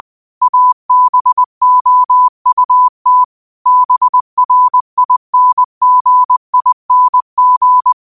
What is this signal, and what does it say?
Morse code, heard as a keyed tone, about 20 wpm, 1000 Hz. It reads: ABOUT BRINGING